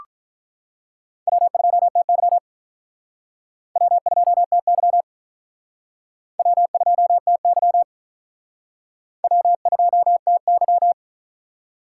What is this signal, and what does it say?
Morse code, heard as a keyed tone, about 48 wpm, 700 Hz. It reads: W2TY W2TY W2TY W2TY